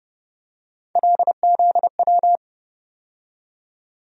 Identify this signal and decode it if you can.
Morse code, heard as a keyed tone, about 30 wpm, 700 Hz. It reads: LZW
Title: USA Call Sign Postfixes - 3 Letters 30wpm